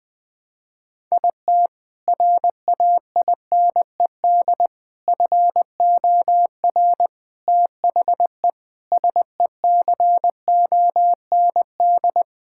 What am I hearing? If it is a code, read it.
Morse code, heard as a keyed tone, about 20 wpm, 700 Hz. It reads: IT RAINED FOR THE SECOND
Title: Sentences with 7-Word Phrases - Spoken Beginning - 20wpm